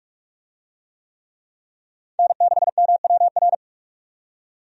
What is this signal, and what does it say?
Morse code, heard as a keyed tone, about 41 wpm, 700 Hz. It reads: N6MWR
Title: USA Call Signs - 5 Characters - 45wpm